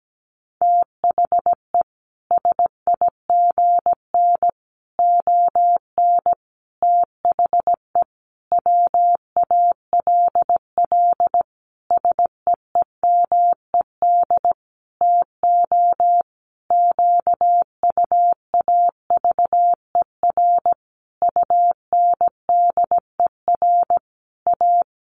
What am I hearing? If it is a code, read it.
Morse code, heard as a keyed tone, about 17 wpm, 700 Hz. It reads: THE SIGN ON THE WALL SEEMED TO QUAVER UNDER A
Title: A Sound Of Thunder 17wpm